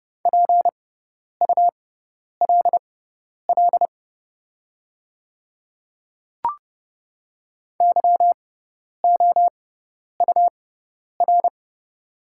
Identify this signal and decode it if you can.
Morse code, heard as a keyed tone, about 31 wpm, spaced as for 10 wpm, 700 Hz. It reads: PULL YOUR
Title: 6x Character Spacing - Top 500 Words - 30wpm